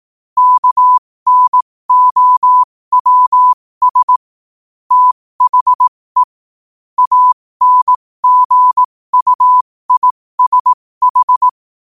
Morse code, heard as a keyed tone, about 18 wpm, spaced as for 15 wpm, 1000 Hz. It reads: KNOWS THE ANGUISH
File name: qotdfiles / 15wpm